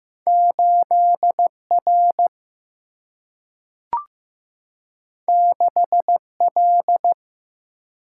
Morse code, heard as a keyed tone, about 15 wpm, 700 Hz. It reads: 8 R 6 L
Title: Single-Letter Single-Number 15wpm